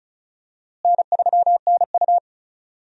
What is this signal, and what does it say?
Morse code, heard as a keyed tone, about 36 wpm, 700 Hz. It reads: N3NU
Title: USA Call Signs - 4 Characters - Repeated 3x - 35wpm